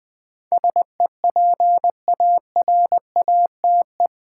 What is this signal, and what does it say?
Morse code, heard as a keyed tone, about 20 wpm, 700 Hz. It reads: SEPARATE
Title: Sentences from Top 1000 Words - Review Part 7 - New Word Drill - 20wpm